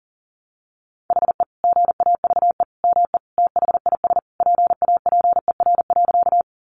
Morse code, heard as a keyed tone, about 40 wpm, 700 Hz. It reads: HE GAVE ME THIS PAPER.
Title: Sentences from Top 500 Words 40wpm